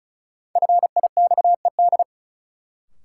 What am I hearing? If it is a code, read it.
Morse code, heard as a keyed tone, about 36 wpm, 700 Hz. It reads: FIXED
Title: Sentences from Top 1100 Words - New Word Drill - 35wpm